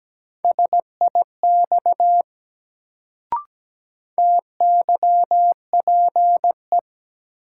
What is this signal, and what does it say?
Morse code, heard as a keyed tone, about 17 wpm, 700 Hz. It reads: SIX TYPE
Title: Sentences from Top 600 Words - New Word Drill - 17wpm